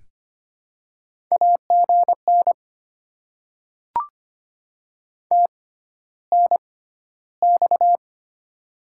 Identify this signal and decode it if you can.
Morse code, heard as a keyed tone, about 25 wpm, spaced as for 13 wpm, 700 Hz. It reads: AGN TNX
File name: Top 20 QSO Elements - 6x Character Spacing - 25wpm